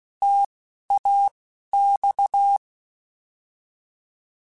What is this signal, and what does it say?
Morse code, heard as a keyed tone, about 16 wpm, spaced as for 11 wpm, 800 Hz. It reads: TAX